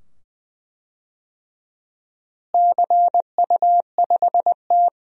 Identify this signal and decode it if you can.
Morse code, heard as a keyed tone, about 20 wpm, 700 Hz. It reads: CU5T
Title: DXCC-100 Call Signs - 4 Characters - Repeated 2x - 20wpm